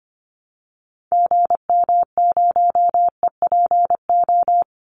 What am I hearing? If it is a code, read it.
Morse code, heard as a keyed tone, about 25 wpm, 700 Hz. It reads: GM0EPO
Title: Call Signs - 25wpm